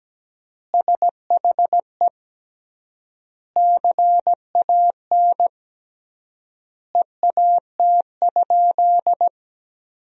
Morse code, heard as a keyed tone, about 17 wpm, 700 Hz. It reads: SHE CAN EAT?
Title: 3x Word-Spacing - Sentences from Top 500 Words - 17wpm